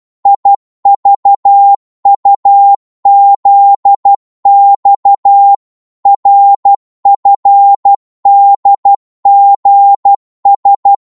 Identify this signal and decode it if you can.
Morse code, heard as a keyed tone, about 12 wpm, 800 Hz. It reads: IVUZX RFDGS
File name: Morse / ltr-12wpm